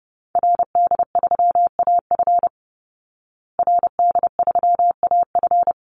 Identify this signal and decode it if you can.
Morse code, heard as a keyed tone, about 30 wpm, 700 Hz. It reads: RD3AF RD3AF
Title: Call Signs Repeated 2x - 30wpm